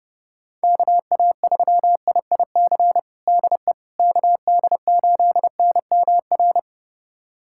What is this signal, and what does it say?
Morse code, heard as a keyed tone, about 30 wpm, 700 Hz. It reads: KA3IIC DE KD8NMR
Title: USA Callsign DE Callsign - 6 Characters - 30wpm